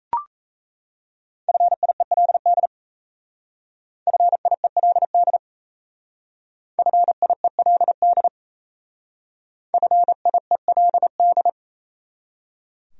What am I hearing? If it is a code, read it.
Morse code, heard as a keyed tone, about 32 wpm, 700 Hz. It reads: FIELD FIELD FIELD FIELD
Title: Speed-Racing Top 500 Words 28wpm